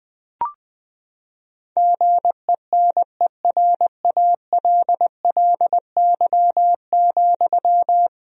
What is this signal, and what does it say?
Morse code, heard as a keyed tone, about 20 wpm, 700 Hz. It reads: GENERALLY,
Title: Sentences from Top 1000 Words - Review Part 4 - 20wpm